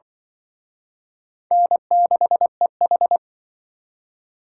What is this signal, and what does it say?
Morse code, heard as a keyed tone, about 24 wpm, 700 Hz. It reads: N6EH
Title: Speed Racing - USA 1x2 Call Signs 20wpm